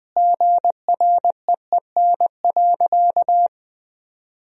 Morse code, heard as a keyed tone, about 20 wpm, 700 Hz. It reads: GREEN.
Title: Sentences from Top 1000 Words - Review Part 2 - 20wpm